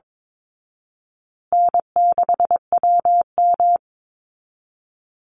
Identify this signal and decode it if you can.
Morse code, heard as a keyed tone, about 22 wpm, 700 Hz. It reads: N6WM
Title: Easy Call Signs Repeated 2x - 22wpm